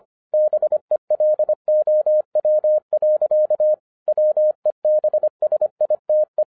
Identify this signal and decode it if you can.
Morse code, heard as a keyed tone, about 25 wpm, 610 Hz. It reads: BELOW. WEBSITE